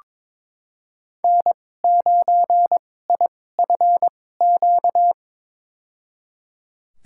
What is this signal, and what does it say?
Morse code, heard as a keyed tone, about 22 wpm, spaced as for 16 wpm, 700 Hz. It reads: N9IFQ
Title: USA Call Signs - 5 Characters - 2x Character Spacing - 22wpm